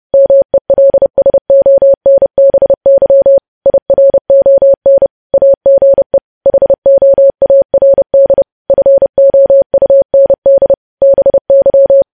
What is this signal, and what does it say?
Morse code, heard as a keyed tone, about 30 wpm, 550 Hz. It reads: MELSONBY IRON AGE HOARD FOUND BY
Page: news headlines in morse code at 30 wpm